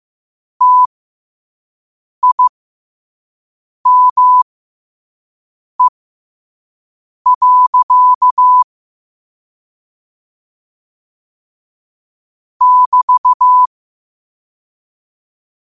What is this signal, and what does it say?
Morse code, heard as a keyed tone, about 15 wpm, spaced as for 5 wpm, 1000 Hz. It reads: TIME. <BT>
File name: qotdfiles / 5wpm